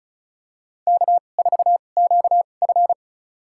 Morse code, heard as a keyed tone, about 36 wpm, spaced as for 25 wpm, 700 Hz. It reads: K4QF
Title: USA Call Signs - 4 Characters - 2x Character Spacing - 35wpm